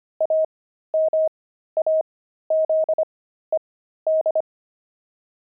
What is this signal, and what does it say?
Morse code, heard as a keyed tone, about 25 wpm, spaced as for 13 wpm, 630 Hz. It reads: AMAZED
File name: pangram-fs-13-cs-25-watches